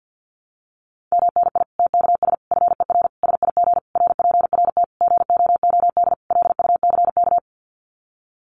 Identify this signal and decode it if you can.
Morse code, heard as a keyed tone, about 50 wpm, 700 Hz. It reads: MNI TKS FER SIG RPRT GOOD LUCK
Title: Common QSO Phrases 50wpm